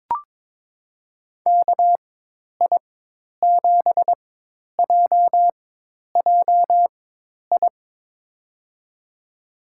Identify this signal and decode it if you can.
Morse code, heard as a keyed tone, about 21 wpm, spaced as for 10 wpm, 700 Hz. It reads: KI7JJI
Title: USA Call Signs - 6 Characters - 4x Character Spacing - 22wpm